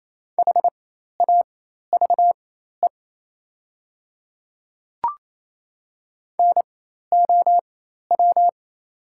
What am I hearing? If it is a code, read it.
Morse code, heard as a keyed tone, about 28 wpm, spaced as for 13 wpm, 690 Hz. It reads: HAVE NOW
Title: 4x Character Spacing - Top 100 Words - 28wpm